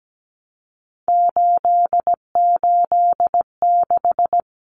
Morse code, heard as a keyed tone, about 17 wpm, 700 Hz. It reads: 886
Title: Numbers - 3-Digit 17wpm